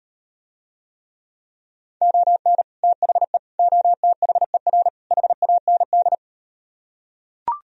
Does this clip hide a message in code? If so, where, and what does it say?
Yes: Morse code from 2.0 to 6.1 s, heard as a keyed tone, about 39 wpm, 700 Hz. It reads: ON THE OTHER HAND